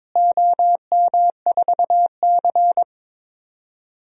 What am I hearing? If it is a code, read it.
Morse code, heard as a keyed tone, about 21 wpm, 700 Hz. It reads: OM4C
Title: DXCC-100 Call Signs - 4 Characters - Repeated 3x - 22wpm